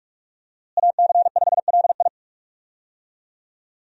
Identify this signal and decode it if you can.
Morse code, heard as a keyed tone, about 42 wpm, 700 Hz. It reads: AC5LI
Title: USA Call Signs - 5 Characters - Repeated 2x - 45wpm